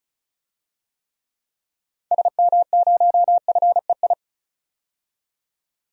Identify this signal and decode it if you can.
Morse code, heard as a keyed tone, about 34 wpm, 700 Hz. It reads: SM0FEI
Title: DXCC-100 Call Signs - 6 Characters - 35wpm